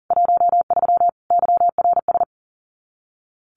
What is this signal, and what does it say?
Morse code, heard as a keyed tone, about 40 wpm, 700 Hz. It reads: 13 YRS
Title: Common QSO Phrases 40wpm